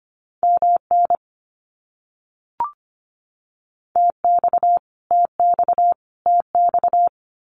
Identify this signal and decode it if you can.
Morse code, heard as a keyed tone, about 25 wpm, 700 Hz. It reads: MN TX TX TX
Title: US States Repeated 3x 25wpm